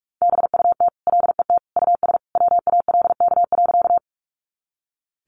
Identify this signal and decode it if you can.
Morse code, heard as a keyed tone, about 44 wpm, 700 Hz. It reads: BUT LET US WALK.
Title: Sentences from Top 300 Words 45wpm